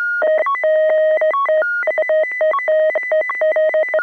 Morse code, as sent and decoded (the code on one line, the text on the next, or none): none